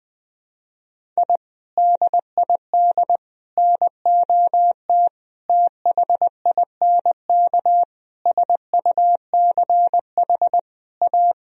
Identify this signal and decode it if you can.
Morse code, heard as a keyed tone, about 21 wpm, 700 Hz. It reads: I DID NOT THINK SUCH A
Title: Sentences from Top 1000 Words - Review Part 1 - 20wpm